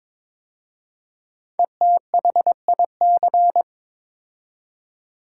Morse code, heard as a keyed tone, about 22 wpm, 700 Hz. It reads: ETHIC
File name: 5 Letter Words - First 3 Letters Given - 22wpm